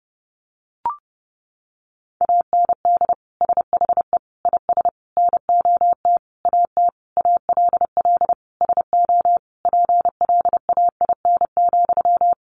Morse code, heard as a keyed tone, about 30 wpm, 700 Hz. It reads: AND SHE IS NOT AT ALL SO PLAIN,
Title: Sentences from Top 600 Words 30wpm